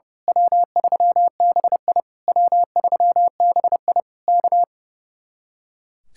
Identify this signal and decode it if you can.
Morse code, heard as a keyed tone, about 30 wpm, 700 Hz. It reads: W3BI W3BI K
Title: Speed-Racing Common QSO Phrases - 30wpm